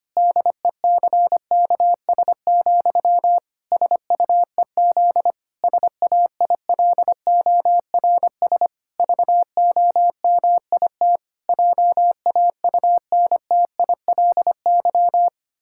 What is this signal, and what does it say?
Morse code, heard as a keyed tone, about 25 wpm, 700 Hz. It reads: DECKS, SUEZ SAILORS VOMIT JAUNTILY
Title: Pangram Challenge v2 - 25wpm